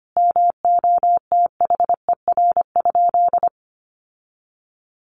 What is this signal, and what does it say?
Morse code, heard as a keyed tone, about 25 wpm, 700 Hz. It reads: MOTHER?
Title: Sentences from Top 1100 Words 25wpm